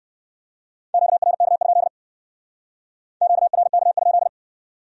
Morse code, heard as a keyed tone, about 59 wpm, 700 Hz. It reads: QRK? QRK?